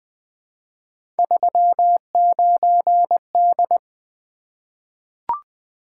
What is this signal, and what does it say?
Morse code, heard as a keyed tone, about 20 wpm, 700 Hz. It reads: 39D
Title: Three Letter-Number Combinations - 20wpm